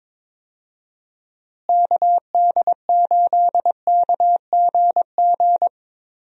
Morse code, heard as a keyed tone, about 22 wpm, 700 Hz. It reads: KD8KGG
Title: USA Call Signs - 6 Characters - Repeated 3x - 22wpm